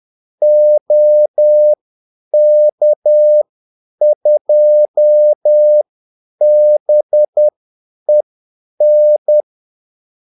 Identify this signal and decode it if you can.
Morse code, heard as a keyed tone, about 10 wpm, spaced as for 8 wpm, 600 Hz. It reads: OK2BEN
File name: Callsigns-10-8wpm